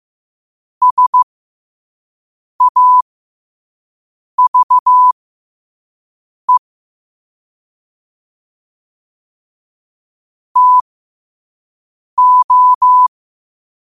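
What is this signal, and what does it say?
Morse code, heard as a keyed tone, about 15 wpm, spaced as for 5 wpm, 1000 Hz. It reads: SAVE TO
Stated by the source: Morse code quote of the day sent at 5 wpm.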